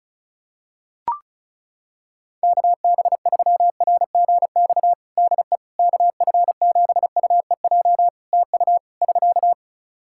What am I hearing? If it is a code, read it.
Morse code, heard as a keyed tone, about 34 wpm, 700 Hz. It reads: KB3RGX DE KF7UEJ TU <SK>
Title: Practice QSO - Exchange 4 - Ending - 35wpm